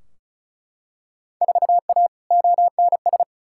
Morse code, heard as a keyed tone, about 36 wpm, 700 Hz. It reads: 4A ONS
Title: Field Day Exchanges - 35wpm